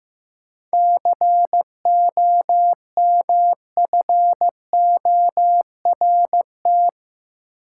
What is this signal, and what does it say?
Morse code, heard as a keyed tone, about 15 wpm, 700 Hz. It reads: COMFORT
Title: Seven Letter Words - 15wpm